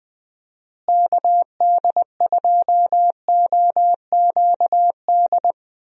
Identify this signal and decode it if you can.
Morse code, heard as a keyed tone, about 20 wpm, 700 Hz. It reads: KD2OQD
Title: USA Call Signs - 6 Characters - Repeated 3x - 20wpm